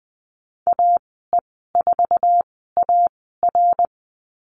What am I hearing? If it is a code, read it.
Morse code, heard as a keyed tone, about 20 wpm, spaced as for 14 wpm, 700 Hz. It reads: AE4AR
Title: 055 Lesson 10g - Letter r - Call Signs 20wpm